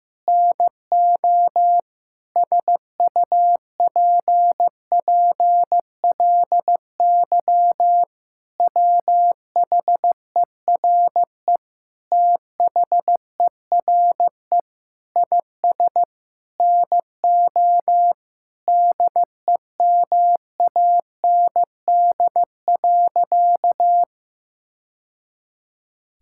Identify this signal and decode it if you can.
Morse code, heard as a keyed tone, about 15 wpm, 700 Hz. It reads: NO SUPPLY WHERE THERE IS NO DEMAND.
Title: Sentences from Top 1000 Words - Review Part 5 - 15wpm